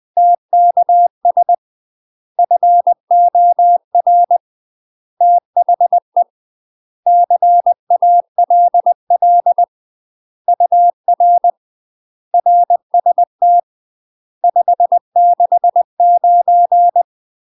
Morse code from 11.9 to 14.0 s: RST